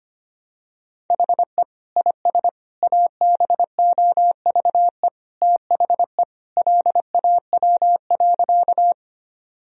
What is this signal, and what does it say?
Morse code, heard as a keyed tone, about 25 wpm, 700 Hz. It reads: HE IS ABOVE THE LAW.
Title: Sentences from Top 1000 Words - Review Part 3 - 25wpm